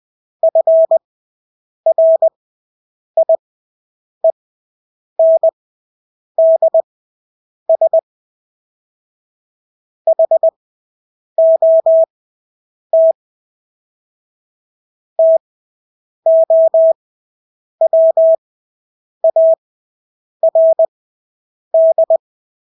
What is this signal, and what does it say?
Morse code, heard as a keyed tone, about 20 wpm, spaced as for 8 wpm, 650 Hz. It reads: FRIENDS HOT TOWARD
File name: Engelske ord 20/8 wpm